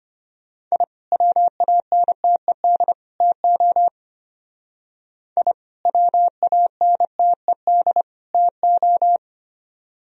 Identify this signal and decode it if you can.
Morse code, heard as a keyed tone, about 33 wpm, 700 Hz. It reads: I WANTED TO I WANTED TO